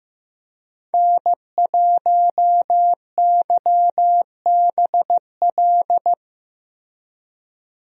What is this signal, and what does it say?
Morse code, heard as a keyed tone, about 15 wpm, 700 Hz. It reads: N1YBL
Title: USA Call Signs - 5 Characters - Repeated 2x - 15wpm